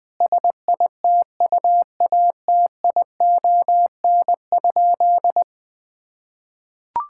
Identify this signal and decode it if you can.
Morse code, heard as a keyed tone, about 20 wpm, 700 Hz. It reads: SITUATION?
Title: Sentences with 5-Word Phrases - 20wpm